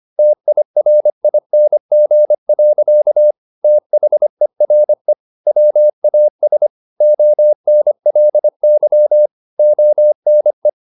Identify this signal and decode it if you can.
Morse code, heard as a keyed tone, about 25 wpm, 600 Hz. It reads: TIRING. THERE WAS ONLY ONE
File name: the old man and the sea part 1 25wpm